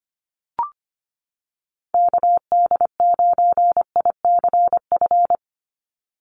Morse code, heard as a keyed tone, about 25 wpm, 690 Hz. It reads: KD9ICF
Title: USA Call Signs 25wpm